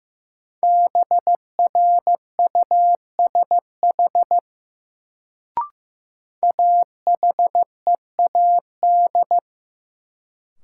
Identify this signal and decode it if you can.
Morse code, heard as a keyed tone, about 15 wpm, 700 Hz. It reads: BRUSH AHEAD
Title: Sentences from Top 1100 Words - New Word Drill Alt - 15wpm